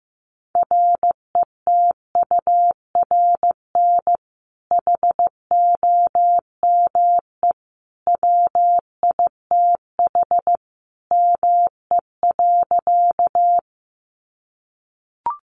Morse code, heard as a keyed tone, about 15 wpm, 700 Hz. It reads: RETURN HOME WITH ME.
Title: Sentences from Top 800 Words 15wpm